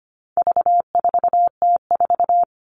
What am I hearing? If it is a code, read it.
Morse code, heard as a keyed tone, about 25 wpm, 700 Hz. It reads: V4T4
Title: All Cut Numbers - 4-Digit 25wpm